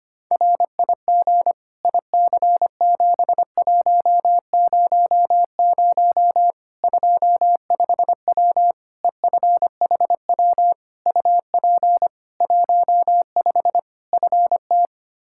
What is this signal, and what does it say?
Morse code, heard as a keyed tone, about 25 wpm, 700 Hz. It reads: RIG IC7100 25W EFHW UP 15 FT
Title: Practice QSO - Exchange 2 - Info - 25wpm